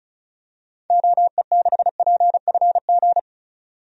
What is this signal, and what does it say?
Morse code, heard as a keyed tone, about 34 wpm, 700 Hz. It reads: OE6PFG
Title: DXCC-100 Call Signs - 6 Characters - 35wpm